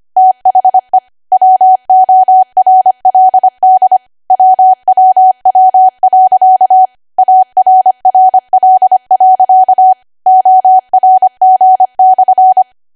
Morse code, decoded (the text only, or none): THE WORLD WWW.ARRL.ORG/